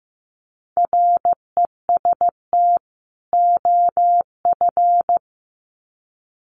Morse code, heard as a keyed tone, about 15 wpm, 700 Hz. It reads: REST OF
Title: Sets of 2 Words 15wpm